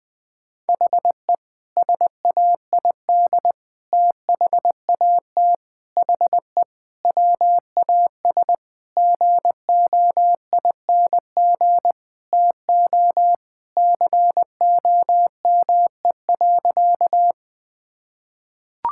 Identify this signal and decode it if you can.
Morse code, heard as a keyed tone, about 20 wpm, 700 Hz. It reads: HE SAID THAT HE WAS GOING TO COME.
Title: Sentences with 5-Word Phrases - 20wpm